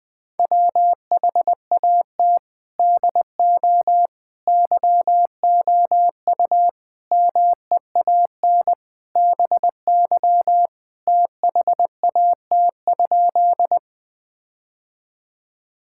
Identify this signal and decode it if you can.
Morse code, heard as a keyed tone, about 20 wpm, 700 Hz. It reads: WHAT DO YOU MEAN BY THAT?
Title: Sentences with 5-Word Phrases - 20wpm